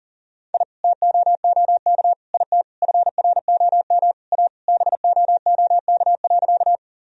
Morse code, heard as a keyed tone, about 37 wpm, 700 Hz. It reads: I TOOK IT FROM A BOOK.
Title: Sentences from Top 1000 Words - Review Part 1 - 40wpm